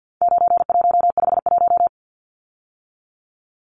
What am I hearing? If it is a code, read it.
Morse code, heard as a keyed tone, about 50 wpm, 700 Hz. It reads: 9151
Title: Numbers - 4-Digit 50wpm